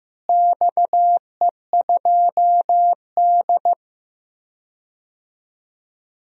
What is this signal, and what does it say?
Morse code, heard as a keyed tone, about 16 wpm, 700 Hz. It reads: XE2D
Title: DXCC-100 Call Signs - 4 Characters - Repeated 3x - 15wpm